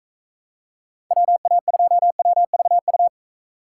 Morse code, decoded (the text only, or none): WA2WVU